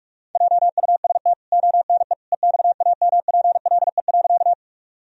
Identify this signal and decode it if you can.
Morse code, heard as a keyed tone, about 44 wpm, 700 Hz. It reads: JUST ONE EXAMPLE.
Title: Sentences from Top 1000 Words - Review Part 1 - 45wpm